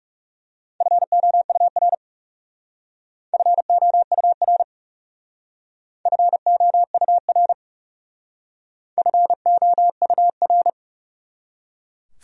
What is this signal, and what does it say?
Morse code, heard as a keyed tone, about 36 wpm, 700 Hz. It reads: FOUR FOUR FOUR FOUR
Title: Speed-Racing Top 500 Words 30wpm